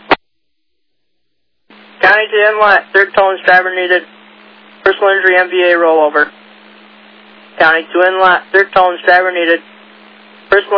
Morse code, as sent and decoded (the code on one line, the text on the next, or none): none